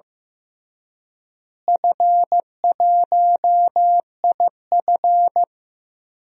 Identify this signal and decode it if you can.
Morse code, heard as a keyed tone, about 16 wpm, 700 Hz. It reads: F1IF
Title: DXCC-100 Call Signs - 4 Characters - Repeated 3x - 15wpm